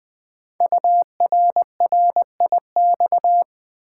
Morse code, heard as a keyed tone, about 20 wpm, 700 Hz. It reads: URRIX
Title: Code Groups - 5 Letters - 20wpm